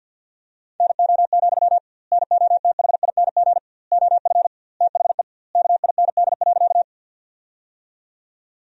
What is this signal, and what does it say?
Morse code, heard as a keyed tone, about 52 wpm, 700 Hz. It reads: NO, NOTHING OF THE KIND.
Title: 2x Word-Spacing - Sentences from Top 500 Words - 50wpm